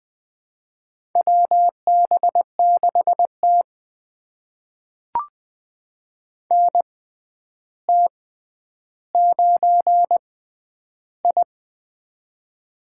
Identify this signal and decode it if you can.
Morse code, heard as a keyed tone, about 21 wpm, spaced as for 10 wpm, 700 Hz. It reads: WB6T NT9I
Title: USA Call Signs - 6x Character Spacing - 20wpm